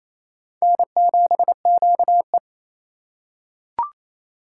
Morse code, heard as a keyed tone, about 28 wpm, 700 Hz. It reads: N7QE
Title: USA Call Signs - 4 Characters - 3x Character Spacing - 28wpm